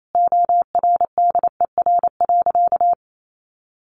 Morse code, heard as a keyed tone, about 28 wpm, 700 Hz. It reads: ORDER.
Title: Sentences from Top 1300 Words 28wpm